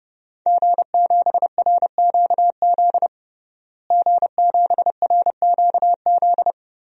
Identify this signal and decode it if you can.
Morse code, heard as a keyed tone, about 30 wpm, 700 Hz. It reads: G7RQZ G7RQZ
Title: DXCC-100 Call Signs - 5 Characters - Repeated 2x - 30wpm